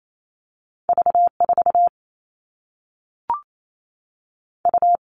Morse code, heard as a keyed tone, about 28 wpm, 700 Hz. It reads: V4 U
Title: All Cut Numbers - 2-Digit 28wpm